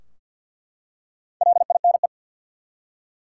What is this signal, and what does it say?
Morse code, heard as a keyed tone, about 49 wpm, 700 Hz. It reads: LINE
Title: Four Letter Words - 50wpm